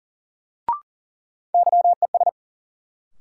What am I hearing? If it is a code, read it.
Morse code, heard as a keyed tone, about 40 wpm, 700 Hz. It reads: YES
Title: Sentences from Top 500 Words - New Word Drill - 40wpm